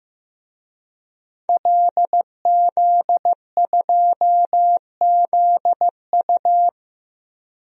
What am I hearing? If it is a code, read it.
Morse code, heard as a keyed tone, about 15 wpm, 700 Hz. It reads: LZ2ZU
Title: DXCC-100 Call Signs - 5 Characters - Repeated 2x - 15wpm